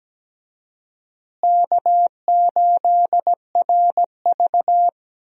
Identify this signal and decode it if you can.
Morse code, heard as a keyed tone, about 17 wpm, 700 Hz. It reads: K8RV
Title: USA Call Signs - 4 Characters - Repeated 2x - 17wpm